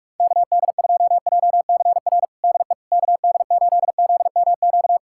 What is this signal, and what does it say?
Morse code, heard as a keyed tone, about 41 wpm, 700 Hz. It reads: KD2JCR DE KD8ZMQ
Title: USA Callsign DE Callsign 45wpm